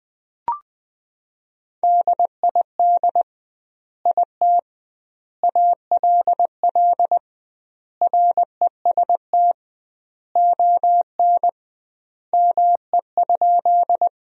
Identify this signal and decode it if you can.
Morse code, heard as a keyed tone, about 20 wpm, 700 Hz. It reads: DID IT ALL REST ON ME?
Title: 2x Word-Spacing - Sentences from Top 500 Words - 20wpm